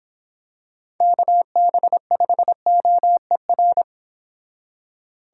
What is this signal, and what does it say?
Morse code, heard as a keyed tone, about 26 wpm, 700 Hz. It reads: KB5OER